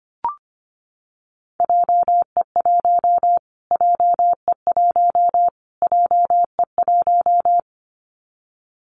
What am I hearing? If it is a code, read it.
Morse code, heard as a keyed tone, about 25 wpm, 700 Hz. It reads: JE1 JE1 JE1
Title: Callsign Prefixes for Top 25 DXCC - Repeated 3x - 25wpm